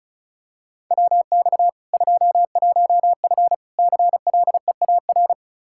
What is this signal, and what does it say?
Morse code, heard as a keyed tone, about 36 wpm, 700 Hz. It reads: WX 21F CLEAR
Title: K1USN SST QSO Practice 35wpm